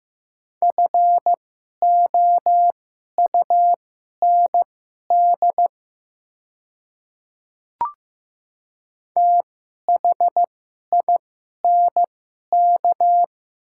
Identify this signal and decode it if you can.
Morse code, heard as a keyed tone, about 15 wpm, spaced as for 11 wpm, 700 Hz. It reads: FOUND THINK
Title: 2x Character Spacing - Top 300 Words - 15wpm